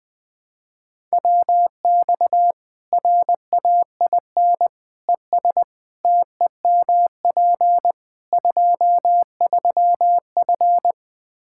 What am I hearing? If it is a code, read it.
Morse code, heard as a keyed tone, about 20 wpm, 700 Hz. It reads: WX RAIN ES TEMP 23F
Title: Practice QSO - Exchange 2 - Info - 20wpm